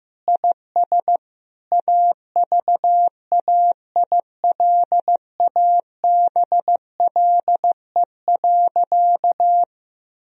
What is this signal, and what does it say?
Morse code, heard as a keyed tone, about 15 wpm, 700 Hz. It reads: IS AVAILABLE.
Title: Sentences from Top 1000 Words - Review Part 3 - 15wpm